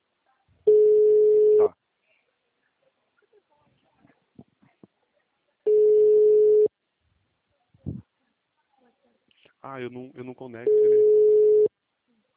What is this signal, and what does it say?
Ringback tone (Europe, 425 Hz), with speech.